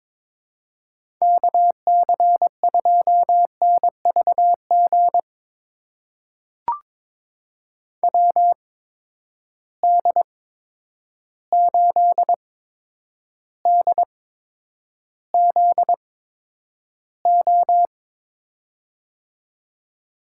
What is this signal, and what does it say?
Morse code, heard as a keyed tone, about 21 wpm, spaced as for 9 wpm, 700 Hz. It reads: KC2NVG WD8DZO
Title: USA Call Signs - 6 Characters - 8x Character Spacing - 22wpm